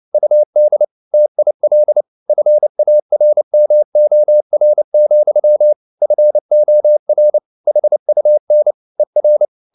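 Morse code, heard as a keyed tone, about 29 wpm, 600 Hz. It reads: UD TIL FARMOR, FOR HUN ER
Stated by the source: Fast (QRQ) Morse at 29 wpm, from CW med Gnister.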